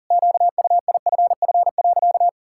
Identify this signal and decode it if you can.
Morse code, heard as a keyed tone, about 40 wpm, 700 Hz. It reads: QUIFF.
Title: Pangram Challenge v2 - 40wpm